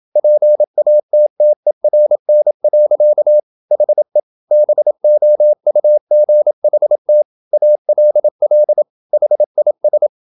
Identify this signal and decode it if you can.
Morse code, heard as a keyed tone, about 27 wpm, 600 Hz. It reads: PATTERN. HE BOUGHT ALL HIS
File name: Never 0027wpm